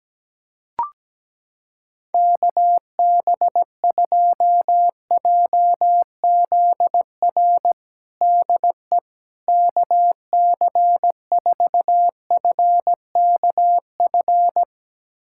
Morse code, heard as a keyed tone, about 17 wpm, 700 Hz. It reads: KB2JZR DE KC4FKF
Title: USA Callsign DE Callsign - 6 Characters - 17wpm